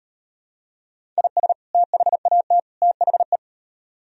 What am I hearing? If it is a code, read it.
Morse code, heard as a keyed tone, about 38 wpm, 700 Hz. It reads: IS THAT THE